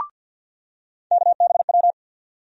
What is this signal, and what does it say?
Morse code, heard as a keyed tone, about 52 wpm, 700 Hz. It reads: KBW
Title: USA Call Sign Postfixes - 3 Letters 50wpm